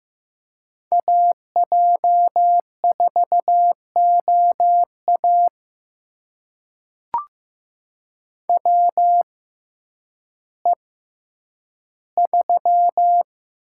Morse code, heard as a keyed tone, about 15 wpm, spaced as for 9 wpm, 700 Hz. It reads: AJ4OA WE3
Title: USA Call Signs - 6x Character Spacing - 15wpm